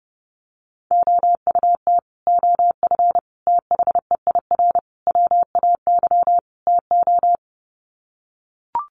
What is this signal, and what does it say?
Morse code, heard as a keyed tone, about 30 wpm, 700 Hz. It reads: OUT OF THEIR WAY TO
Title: Sets of 5 Words Encore2 30wpm